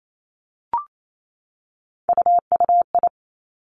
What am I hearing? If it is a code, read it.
Morse code, heard as a keyed tone, about 28 wpm, 700 Hz. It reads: UUI
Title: Three Challenging Letter-Number Combinations 28wpm